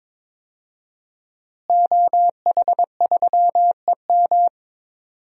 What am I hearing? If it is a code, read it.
Morse code, heard as a keyed tone, about 22 wpm, 700 Hz. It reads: OH3EM
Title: DXCC-100 Call Signs - 5 Characters - Repeated 3x - 22wpm